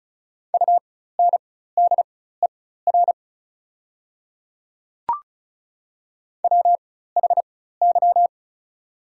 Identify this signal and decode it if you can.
Morse code, heard as a keyed tone, about 36 wpm, spaced as for 16 wpm, 700 Hz. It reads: UNDER WHY
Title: 4x Character Spacing - Top 200 Words - 35wpm